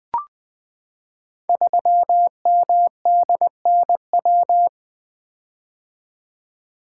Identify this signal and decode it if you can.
Morse code, heard as a keyed tone, about 20 wpm, 700 Hz. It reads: 3MDNW
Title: Code Groups - 5 Letter:Numbers - 20wpm